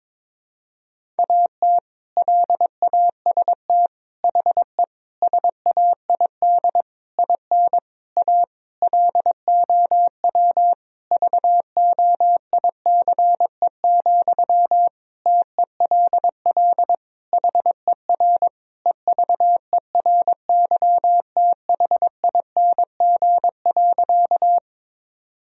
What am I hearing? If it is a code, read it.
Morse code, heard as a keyed tone, about 22 wpm, 700 Hz. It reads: AT LAST HE SAID IN A LOW VOICE, TELL HER EVERYTHING.
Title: Sentences from Top 1000 Words - Review Part 1 - 22wpm